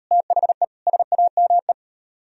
Morse code, heard as a keyed tone, about 34 wpm, 700 Hz. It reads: THE SAME